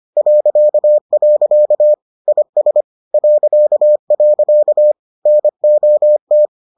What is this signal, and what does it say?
Morse code, heard as a keyed tone, about 25 wpm, 600 Hz. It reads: .. IS .. NOT